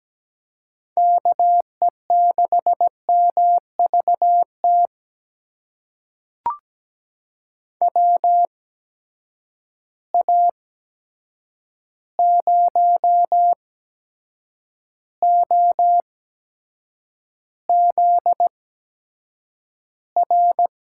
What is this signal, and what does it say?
Morse code, heard as a keyed tone, about 17 wpm, spaced as for 7 wpm, 700 Hz. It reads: KE6MVT WA0OZR
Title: USA Call Signs - 6 Characters - 8x Character Spacing - 17wpm